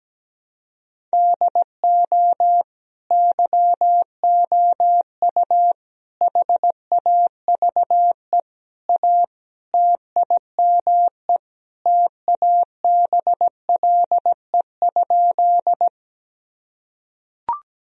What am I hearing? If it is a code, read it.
Morse code, heard as a keyed tone, about 17 wpm, 700 Hz. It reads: DO YOU HAVE A TIME TABLE?
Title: Sentences from Top 1000 Words - Review Part 2 - 17wpm